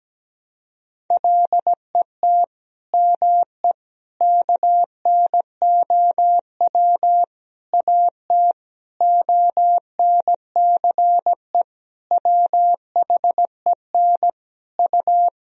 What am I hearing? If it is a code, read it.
Morse code, heard as a keyed tone, about 17 wpm, 700 Hz. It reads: LET ME KNOW AT ONCE WHEN U
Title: Sentences from Top 300 Words and Abbreviations 17wpm